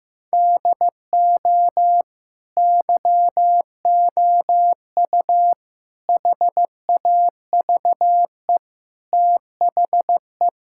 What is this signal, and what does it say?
Morse code, heard as a keyed tone, about 15 wpm, 700 Hz. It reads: DO YOU HAVE THE
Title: Sentences from Top 1000 Words - Review Part 5 - 15wpm